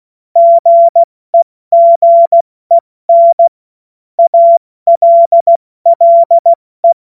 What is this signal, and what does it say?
Morse code, heard as a keyed tone, about 16 wpm, 680 Hz. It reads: GEGEN ALLE
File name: Morsekurs / Witze / Witz 30. August 2022 70 F80